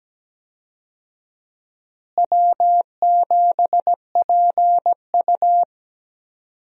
Morse code, heard as a keyed tone, about 17 wpm, 700 Hz. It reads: W7PU
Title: USA Call Signs - 4 Characters - Repeated 2x - 17wpm